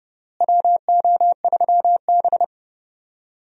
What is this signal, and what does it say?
Morse code, heard as a keyed tone, about 30 wpm, 700 Hz. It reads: WO3B
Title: USA Call Signs - 4 Characters - Repeated 3x - 30wpm